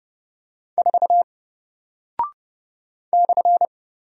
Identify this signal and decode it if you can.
Morse code, heard as a keyed tone, about 30 wpm, 700 Hz. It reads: 4 /
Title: Single Character (Letter:Number:Punctuation:Prosign) Alt 30wpm